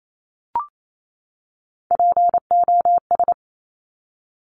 Morse code, heard as a keyed tone, about 28 wpm, 700 Hz. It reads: POS
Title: Top Three Letter Combinations in Words 28wpm